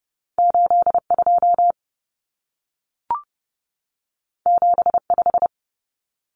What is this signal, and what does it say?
Morse code, heard as a keyed tone, about 30 wpm, 700 Hz. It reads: 82 75
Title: Common Cut Numbers - 2-Digit 30wpm